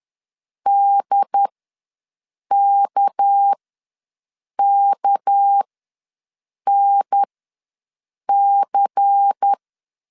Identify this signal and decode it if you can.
Morse code, heard as a keyed tone, about 10 wpm, spaced as for 6 wpm, 800 Hz. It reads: DKKNC
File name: LEZIONI / 6 PKX0 / 6 PK ws 10wpm 3745